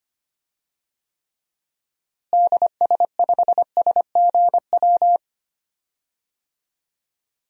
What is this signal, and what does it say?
Morse code, heard as a keyed tone, about 25 wpm, 700 Hz. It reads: DS5SGW
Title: DXCC-100 Call Signs - 6 Characters - Repeated 3x - 25wpm